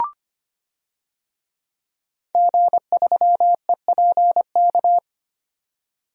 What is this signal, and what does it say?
Morse code, heard as a keyed tone, about 25 wpm, 700 Hz. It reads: G3EPK
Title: DXCC-100 Call Signs - 5 Characters - 25wpm